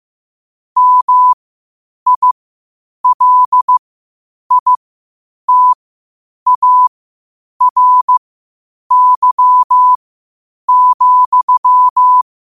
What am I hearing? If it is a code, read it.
Morse code, heard as a keyed tone, about 15 wpm, spaced as for 8 wpm, 1000 Hz. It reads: MILITARY,
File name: qotdfiles / 8wpm